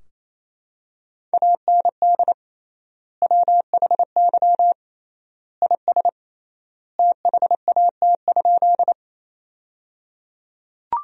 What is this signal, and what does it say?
Morse code, heard as a keyed tone, about 28 wpm, 700 Hz. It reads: AND WHY IS THAT?
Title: 3x Word-Spacing - Sentences from Top 200 Words - 28wpm